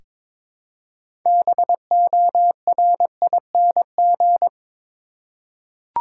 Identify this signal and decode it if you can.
Morse code, heard as a keyed tone, about 22 wpm, 700 Hz. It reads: BORING
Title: Six Letter Words - 22wpm